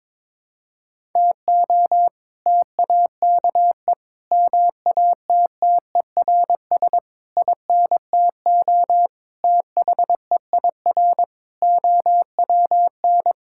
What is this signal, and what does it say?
Morse code, heard as a keyed tone, about 22 wpm, 700 Hz. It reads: TO TAKE MATTERS INTO THEIR OWN
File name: practice / 22wpm / Sets of 7 Words - 22wpm